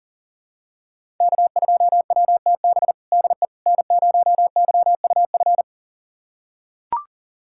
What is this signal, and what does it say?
Morse code, heard as a keyed tone, about 40 wpm, 700 Hz. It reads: K2WTB DE N0YUF
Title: USA Callsign DE Callsign - 5 Characters - 40wpm